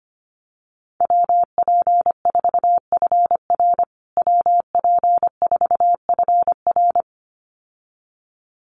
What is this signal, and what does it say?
Morse code, heard as a keyed tone, about 25 wpm, 700 Hz. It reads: WP4FR WP4FR
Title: USA Call Signs Repeated 2x 25wpm